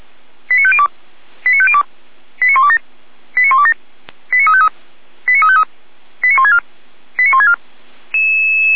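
Five-tone selective call (ZVEI), heard as quick runs of five tones, tones of about 70 ms. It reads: 89502 89502 89127 89127 89353 89353 89164 89164
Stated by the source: ZVEI five-tone call received on an SDR, from the feuerwehr-orlamuende.org (fire brigade Orlamünde) recording server.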